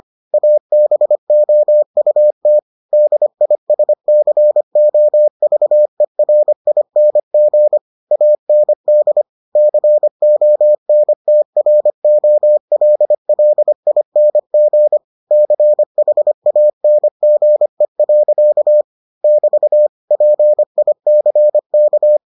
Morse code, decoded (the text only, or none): ABOUT DISCOVERING AND CONTROLLING CHANGE. <BT> PICK